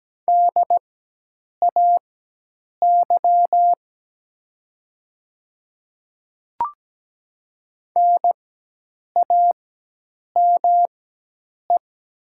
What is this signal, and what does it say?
Morse code, heard as a keyed tone, about 17 wpm, spaced as for 8 wpm, 700 Hz. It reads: DAY NAME
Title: 4x Character Spacing - Top 500 Words - 17wpm